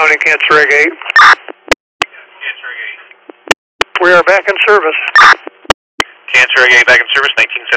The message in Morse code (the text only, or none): none